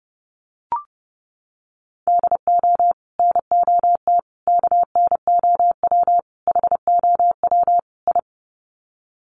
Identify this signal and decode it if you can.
Morse code, heard as a keyed tone, about 30 wpm, 700 Hz. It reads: DO NOT KNOW HOW I
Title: Sets of 5 Words Encore2 30wpm